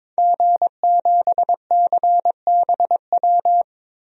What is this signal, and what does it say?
Morse code, heard as a keyed tone, about 22 wpm, 700 Hz. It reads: G7CBW
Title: DXCC-100 Call Signs - 5 Characters - 22wpm